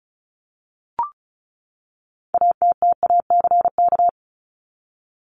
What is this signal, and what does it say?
Morse code, heard as a keyed tone, about 34 wpm, 700 Hz. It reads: ATTACK
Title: Top 1251-1500 Words 35wpm